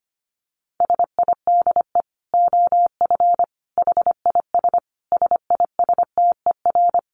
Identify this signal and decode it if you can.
Morse code, heard as a keyed tone, about 25 wpm, 700 Hz. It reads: SIDE OF HIS SISTER
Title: Dit Challenge 25wpm